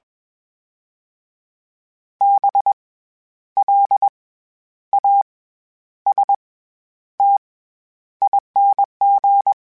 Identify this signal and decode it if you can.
Morse code, heard as a keyed tone, about 21 wpm, spaced as for 10 wpm, 800 Hz. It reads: BLASTING
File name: Special INGs-mp3s / Special INGs-mp3s / SlowToFaster ing @20